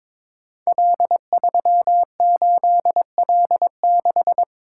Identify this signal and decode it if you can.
Morse code, heard as a keyed tone, about 22 wpm, 700 Hz. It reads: L38L6
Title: Code Groups - 5 Letter:Numbers - 22wpm